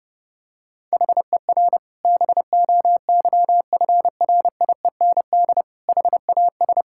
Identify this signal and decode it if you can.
Morse code, heard as a keyed tone, about 32 wpm, 700 Hz. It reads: HER BOYFRIEND HAS
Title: Sentences from Top 1000 Words - Review Part 5 - 30wpm